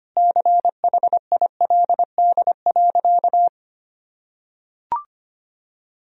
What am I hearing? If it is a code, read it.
Morse code, heard as a keyed tone, about 25 wpm, 700 Hz. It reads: CHILD.
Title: Sentences from Top 1000 Words - Review Part 2 - 25wpm